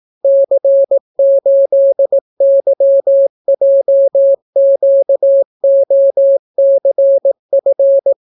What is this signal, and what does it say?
Morse code, heard as a keyed tone, about 18 wpm, 550 Hz. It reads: C8YJQOCF